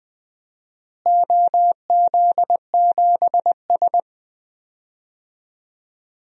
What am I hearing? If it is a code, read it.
Morse code, heard as a keyed tone, about 20 wpm, 700 Hz. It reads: OZ7S
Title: DXCC-100 Call Signs - 4 Characters - Repeated 2x - 20wpm